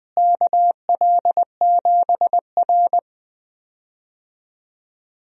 Morse code, heard as a keyed tone, about 20 wpm, 700 Hz. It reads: KL7R
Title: DXCC-100 Call Signs - 4 Characters - Repeated 3x - 20wpm